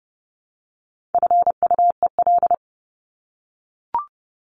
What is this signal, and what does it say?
Morse code, heard as a keyed tone, about 30 wpm, 700 Hz. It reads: FUEL
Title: Top 1751-2000 Words 30wpm